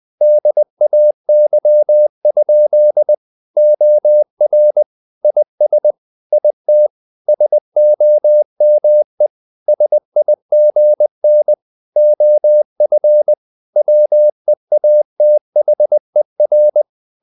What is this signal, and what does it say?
Morse code, heard as a keyed tone, about 20 wpm, 600 Hz. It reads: DAY? OR IS IT SOME SIGN OF WEATHER
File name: the old man and the sea / the old man and the sea part 1 20wpm